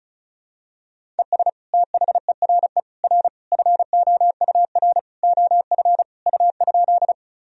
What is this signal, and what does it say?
Morse code, heard as a keyed tone, about 36 wpm, 700 Hz. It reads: ES THERE R FOUR OF U?
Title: Sentences from Top 300 Words and Abbreviations 35wpm